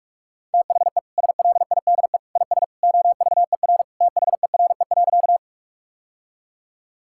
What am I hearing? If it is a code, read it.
Morse code, heard as a keyed tone, about 47 wpm, 700 Hz. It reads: THE SLIDE IS OVER THERE.
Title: Sentences from Top 1600 Words 45wpm